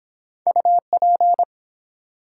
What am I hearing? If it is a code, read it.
Morse code, heard as a keyed tone, about 23 wpm, 700 Hz. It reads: UP